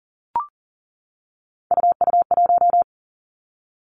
Morse code, heard as a keyed tone, about 38 wpm, 700 Hz. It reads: UU1
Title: Three Challenging Letter-Number Combinations 40wpm